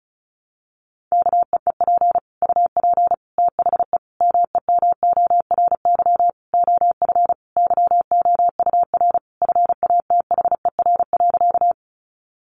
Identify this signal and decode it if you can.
Morse code, heard as a keyed tone, about 34 wpm, 700 Hz. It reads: KEEP UP THE MEMORY OF YOUR FATHER.
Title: Sentences from Top 1300 Words 35wpm